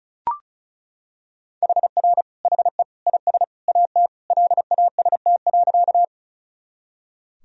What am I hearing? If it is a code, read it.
Morse code, heard as a keyed tone, about 34 wpm, 700 Hz. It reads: HR HE IS AT LAST.
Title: Sentences from Top 300 Words and Abbreviations 35wpm